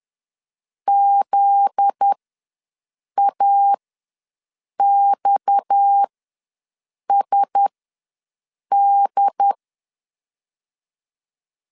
Morse code, heard as a keyed tone, about 10 wpm, spaced as for 6 wpm, 800 Hz. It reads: ZAXSD
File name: LEZIONI / 9 28 ws 10wpm 4124